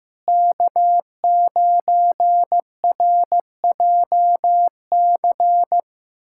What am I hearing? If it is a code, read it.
Morse code, heard as a keyed tone, about 16 wpm, 700 Hz. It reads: K9RJC
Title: USA Callsign DE Callsign - 5 Characters - 15wpm